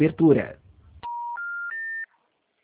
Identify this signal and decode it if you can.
Special information tone, with speech.